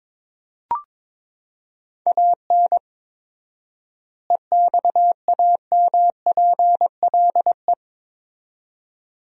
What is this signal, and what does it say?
Morse code, heard as a keyed tone, about 21 wpm, 700 Hz. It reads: AN EXAMPLE
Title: 4x Word-Spacing - Sentences from Top 400 Words - 22wpm